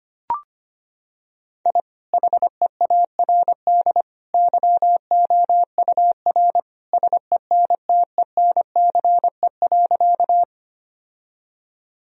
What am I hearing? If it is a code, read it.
Morse code, heard as a keyed tone, about 25 wpm, 700 Hz. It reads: I HEARD YOUR SENTENCE.
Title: Sentences from Top 1000 Words - Review Part 1 - 25wpm